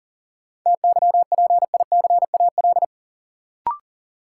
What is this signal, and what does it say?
Morse code, heard as a keyed tone, about 40 wpm, 700 Hz. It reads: TYPICAL
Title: Sentences from Top 1100 Words - New Word Drill - 40wpm